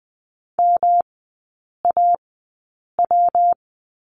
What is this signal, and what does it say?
Morse code, heard as a keyed tone, about 20 wpm, spaced as for 8 wpm, 700 Hz. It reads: MAW
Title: 138 Lesson 23e - Letter f - Three Characters 20wpm